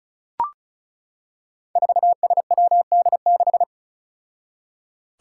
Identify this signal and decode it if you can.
Morse code, heard as a keyed tone, about 36 wpm, 700 Hz. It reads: 4SWD6
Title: Five Challenging Letter-Number Combinations 35wpm